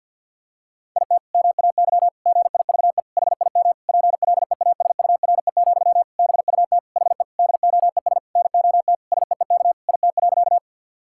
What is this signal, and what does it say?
Morse code, heard as a keyed tone, about 50 wpm, 700 Hz. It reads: IT MAY GIVE HIM PLEASURE, BUT HE DOES NOT SEEK IT.
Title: Sentences from Top 1600 Words 50wpm